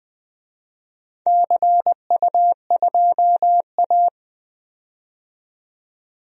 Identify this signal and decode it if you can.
Morse code, heard as a keyed tone, about 20 wpm, 700 Hz. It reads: CU2A
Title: DXCC-100 Call Signs - 4 Characters - 20wpm